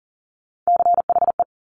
Morse code, heard as a keyed tone, about 38 wpm, 700 Hz. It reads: CHE
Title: Top Three Letter Combinations in Words 40wpm